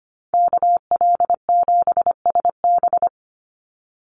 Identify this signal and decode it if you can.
Morse code, heard as a keyed tone, about 25 wpm, 700 Hz. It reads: KL7SB
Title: Call Signs Repeated 2x - 25wpm